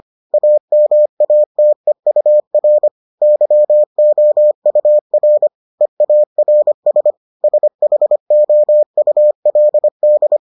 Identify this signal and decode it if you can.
Morse code, heard as a keyed tone, about 25 wpm, 600 Hz. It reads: AMATEUR YOUR EARS SHOULD